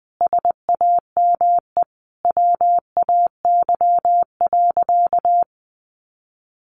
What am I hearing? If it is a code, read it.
Morse code, heard as a keyed tone, about 20 wpm, 700 Hz. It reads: SAME WAY.
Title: Sentences from Top 1300 Words 20wpm